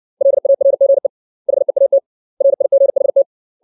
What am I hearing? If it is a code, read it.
Morse code, heard as a keyed tone, about 60 wpm, 550 Hz. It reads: LARGE HEAT RIGHT
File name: 632 words / 632 Words 60WPM